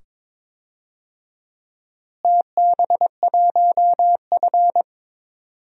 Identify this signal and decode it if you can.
Morse code, heard as a keyed tone, about 22 wpm, 700 Hz. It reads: TB1F
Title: DXCC-100 Call Signs - 4 Characters - Repeated 2x - 22wpm